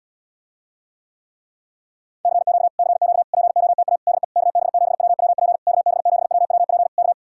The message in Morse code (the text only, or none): CQ CQ POTA DE KB7WWP KB7WWP K